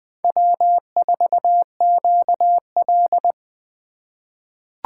Morse code, heard as a keyed tone, about 20 wpm, 700 Hz. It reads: W4QL
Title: USA Call Signs - 4 Characters - 8x Character Spacing - 20wpm